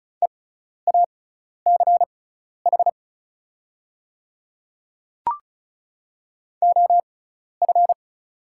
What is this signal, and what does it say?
Morse code, heard as a keyed tone, about 36 wpm, spaced as for 12 wpm, 690 Hz. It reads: EACH OF
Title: 6x Character Spacing - Top 200 Words - 35wpm